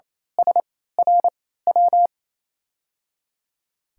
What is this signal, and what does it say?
Morse code, heard as a keyed tone, about 27 wpm, spaced as for 16 wpm, 700 Hz. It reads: SRW
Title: USA Call Signs - 3x Character Spacing - 28wpm